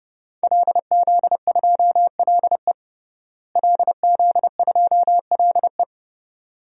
Morse code, heard as a keyed tone, about 30 wpm, 700 Hz. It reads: LZ2LE LZ2LE
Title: DXCC-100 Call Signs - 5 Characters - Repeated 2x - 30wpm